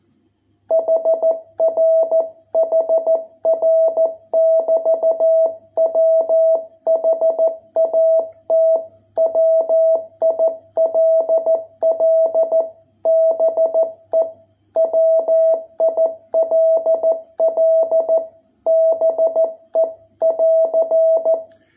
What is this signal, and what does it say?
Morse code, heard as a keyed tone, about 14 wpm, 660 Hz. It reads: HR HR <BT> WHAT WILL BE WILL BE <AR>